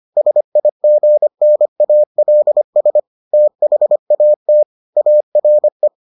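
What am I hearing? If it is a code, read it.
Morse code, heard as a keyed tone, about 25 wpm, 600 Hz. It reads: SIGNALS THAT ARE
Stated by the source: Morse edition of the Foundations of Amateur Radio episodes.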